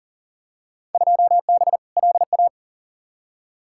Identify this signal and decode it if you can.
Morse code, heard as a keyed tone, about 37 wpm, 700 Hz. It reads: 2B LA
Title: Field Day Exchanges Encore - 40wpm